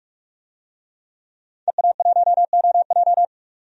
Morse code, heard as a keyed tone, about 48 wpm, 700 Hz. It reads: EA1OJ
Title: DXCC-100 Call Signs - 5 Characters - 45wpm